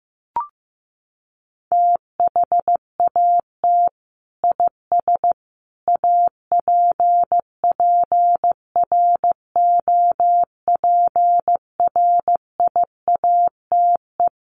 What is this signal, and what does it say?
Morse code, heard as a keyed tone, about 14 wpm, 700 Hz. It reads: THAT IS APPROPRIATE
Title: Sentences from Top 900 Words 15wpm